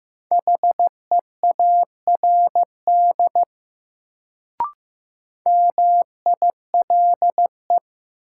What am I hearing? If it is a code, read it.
Morse code, heard as a keyed tone, about 15 wpm, 700 Hz. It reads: HEARD MILE
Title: Sentences from Top 400 Words - New Word Drill - 15wpm